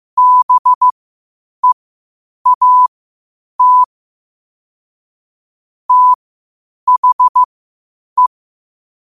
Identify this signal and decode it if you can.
Morse code, heard as a keyed tone, about 15 wpm, spaced as for 8 wpm, 1000 Hz. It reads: BEAT THE
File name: qotdfiles / 8wpm